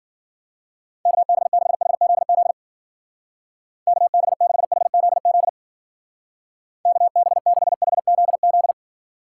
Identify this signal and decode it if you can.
Morse code, heard as a keyed tone, about 48 wpm, 700 Hz. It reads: KB6HZZ KB6HZZ KB6HZZ
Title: Speed Racing - USA 2x3 Call Signs 40wpm